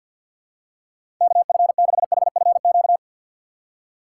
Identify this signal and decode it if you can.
Morse code, heard as a keyed tone, about 46 wpm, 700 Hz. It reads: KF6HFQ
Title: USA Call Signs - 6 Characters - 4x Character Spacing - 50wpm